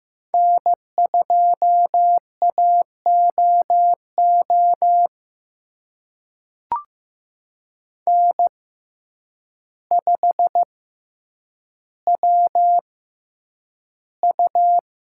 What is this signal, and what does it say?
Morse code, heard as a keyed tone, about 15 wpm, spaced as for 8 wpm, 700 Hz. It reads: N2AOO N5WU
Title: USA Call Signs - 6x Character Spacing - 15wpm